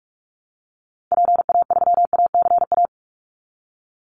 Morse code, heard as a keyed tone, about 46 wpm, 700 Hz. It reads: PA3ACA
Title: Call Signs - 45wpm